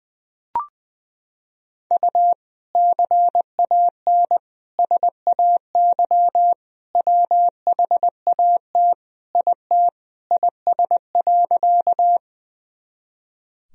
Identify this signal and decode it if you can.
Morse code, heard as a keyed tone, about 21 wpm, 700 Hz. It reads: U CAN SAY WHAT IT IS.
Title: Sentences from Top 200 Words and Abbreviations 20wpm